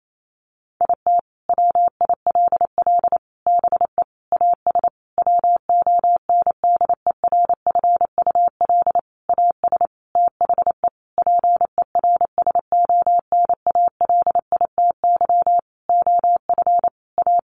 Morse code, heard as a keyed tone, about 28 wpm, 700 Hz. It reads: IT WILL BE AS WONDERFUL AS THE PERSONALITY OF A
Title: Sentences from Top 1400 Words 28wpm